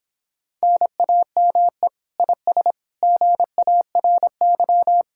Morse code, heard as a keyed tone, about 26 wpm, 700 Hz. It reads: NAME IS GARY